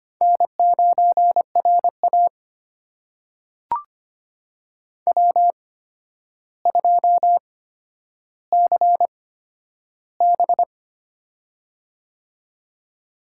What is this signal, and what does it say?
Morse code, heard as a keyed tone, about 25 wpm, spaced as for 11 wpm, 700 Hz. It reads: N9RA W2CB
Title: USA Call Signs - 8x Character Spacing - 25wpm